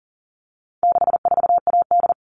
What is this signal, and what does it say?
Morse code, heard as a keyed tone, about 40 wpm, 700 Hz. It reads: 64AD
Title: All Cut Numbers - 4-Digit 40wpm